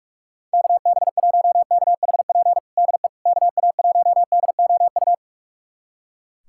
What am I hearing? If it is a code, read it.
Morse code, heard as a keyed tone, about 41 wpm, 700 Hz. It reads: KB1KHP DE KA1DOU
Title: USA Callsign DE Callsign 45wpm